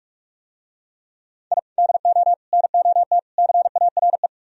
Morse code, heard as a keyed tone, about 44 wpm, 700 Hz. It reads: I DO NOT CARE